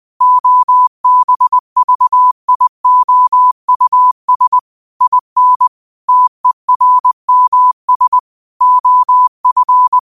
Morse code, heard as a keyed tone, about 20 wpm, 1000 Hz. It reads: OBVIOUS IN TERMS OF
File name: qotdfiles / 20wpm